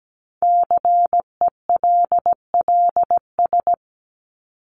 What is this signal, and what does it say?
Morse code, heard as a keyed tone, about 17 wpm, 700 Hz. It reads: CELLS
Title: ES:S Suffix - Top 100 Words with Base Words 17wpm